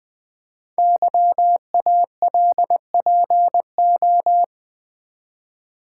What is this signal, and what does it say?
Morse code, heard as a keyed tone, about 20 wpm, 700 Hz. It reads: YALPO
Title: Code Groups - 5 Letters - 20wpm